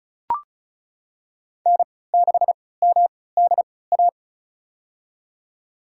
Morse code, heard as a keyed tone, about 34 wpm, spaced as for 20 wpm, 700 Hz. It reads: N6MDA
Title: USA Call Signs - 5 Characters - 3x Character Spacing - 35wpm